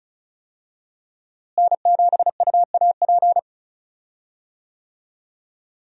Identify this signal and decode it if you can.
Morse code, heard as a keyed tone, about 34 wpm, 700 Hz. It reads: N7UAP
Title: USA Call Signs - 5 Characters - Repeated 3x - 35wpm